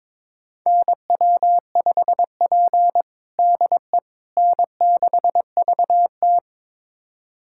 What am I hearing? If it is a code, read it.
Morse code, heard as a keyed tone, about 22 wpm, 700 Hz. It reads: NW5P DE N6VT
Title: USA Callsign DE Callsign - 4 Characters - 22wpm